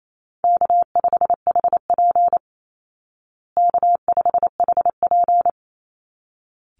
Morse code, heard as a keyed tone, about 28 wpm, 700 Hz. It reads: K5HP K5HP
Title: Call Signs Repeated 3x - 28wpm